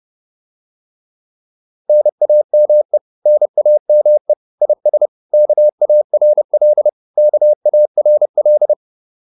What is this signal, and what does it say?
Morse code, heard as a keyed tone, about 30 wpm, 600 Hz. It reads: NAME NAME IS KARL KARL
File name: QSO402 30